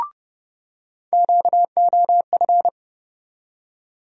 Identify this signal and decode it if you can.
Morse code, heard as a keyed tone, about 30 wpm, 700 Hz. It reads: QOF
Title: USA Call Sign Postfixes - 3 Letters 30wpm